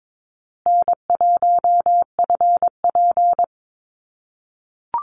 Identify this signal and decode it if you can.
Morse code, heard as a keyed tone, about 22 wpm, 700 Hz. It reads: N1FP
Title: USA Call Signs Repeated 3x 22wpm